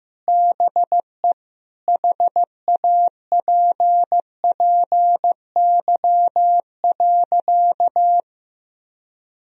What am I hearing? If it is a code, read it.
Morse code, heard as a keyed tone, about 15 wpm, 700 Hz. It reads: BE HAPPY.
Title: Sentences from Top 1000 Words - Review Part 3 - 15wpm